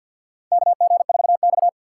Morse code, heard as a keyed tone, about 52 wpm, 700 Hz. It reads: KG4X
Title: USA Call Signs - 4 Characters - 2x Character Spacing - 50wpm